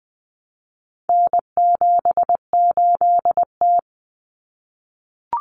Morse code, heard as a keyed tone, about 20 wpm, 700 Hz. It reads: N78T
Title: Common Cut Numbers - 4-Digit 20wpm